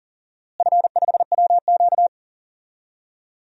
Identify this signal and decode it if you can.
Morse code, heard as a keyed tone, about 40 wpm, 700 Hz. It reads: F5WQ
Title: DXCC-100 Call Signs - 4 Characters - Repeated 3x - 40wpm